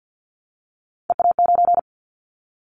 Morse code, heard as a keyed tone, about 50 wpm, 700 Hz. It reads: EA9
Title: Callsign Prefixes for Top 100 DXCC - 50wpm